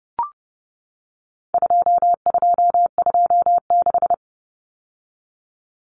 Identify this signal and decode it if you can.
Morse code, heard as a keyed tone, about 30 wpm, 700 Hz. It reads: 2226
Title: Common Cut Numbers - 4-Digit 30wpm